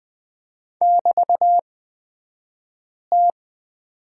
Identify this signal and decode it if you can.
Morse code, heard as a keyed tone, about 20 wpm, spaced as for 5 wpm, 700 Hz. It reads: <BT>T
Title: 223 Lesson 36d - BT - Two Characters 20wpm